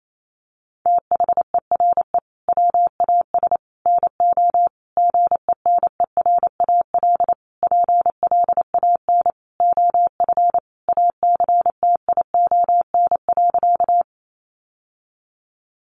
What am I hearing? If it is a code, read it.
Morse code, heard as a keyed tone, about 28 wpm, 700 Hz. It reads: THERE WAS NO GENERAL PLAN OF ACTION.
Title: Sentences from Top 700 Words 28wpm